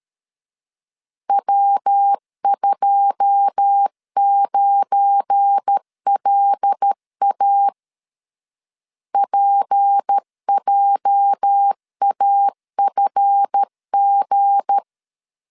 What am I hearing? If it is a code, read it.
Morse code, heard as a keyed tone, about 13 wpm, 800 Hz. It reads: W29LA PJAFG
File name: LEZIONI / 9 2 12wpm 2301